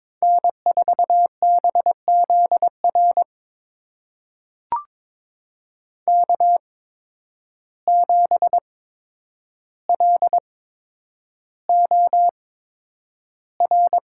Morse code, heard as a keyed tone, about 22 wpm, spaced as for 9 wpm, 700 Hz. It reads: N4BZR K7LOR
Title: USA Call Signs - 5 Characters - 8x Character Spacing - 22wpm